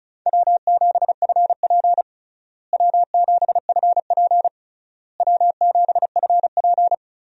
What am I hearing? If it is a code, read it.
Morse code, heard as a keyed tone, about 36 wpm, 700 Hz. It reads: W7FP W7FP W7FP
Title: USA Call Signs - 4 Characters - Repeated 3x - 35wpm